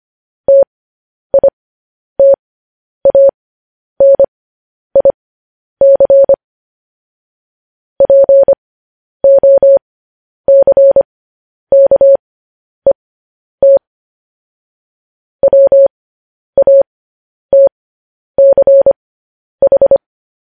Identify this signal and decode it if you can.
Morse code, heard as a keyed tone, about 25 wpm, spaced as for 10 wpm, 550 Hz. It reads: TITANIC POCKET WATCH